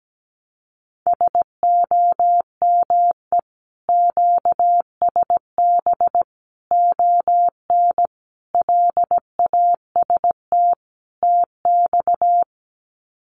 Morse code, heard as a keyed tone, about 17 wpm, 700 Hz. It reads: SOME QSB ON LAST TX
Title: Common QSO Phrases 17wpm